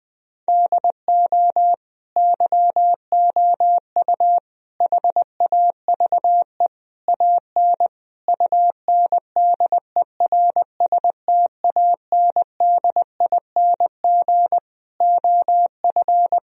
Morse code, heard as a keyed tone, about 20 wpm, 700 Hz. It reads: DO YOU HAVE AN UNDERSTANDING OF
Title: Sentences from Top 1000 Words - Review Part 3 - 20wpm